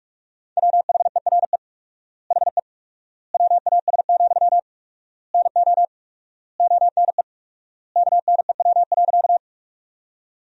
Morse code, heard as a keyed tone, about 41 wpm, 700 Hz. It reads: WHERE HE WAS, NO ONE KNEW.
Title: 4x Word-Spacing - Sentences from Top 400 Words - 45wpm